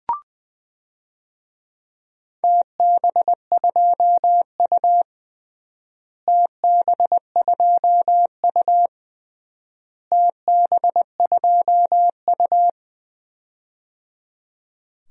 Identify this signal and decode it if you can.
Morse code, heard as a keyed tone, about 21 wpm, 700 Hz. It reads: TB2U TB2U TB2U
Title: DXCC-100 Call Signs - 4 Characters - Repeated 3x - 20wpm